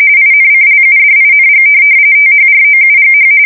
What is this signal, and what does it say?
RTTY, about 45.45 baud. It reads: INCLUDING THROUGH